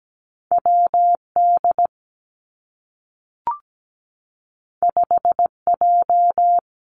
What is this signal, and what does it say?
Morse code, heard as a keyed tone, about 17 wpm, 700 Hz. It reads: WD 5J
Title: Two Challenging Letter-Number Combinations 17wpm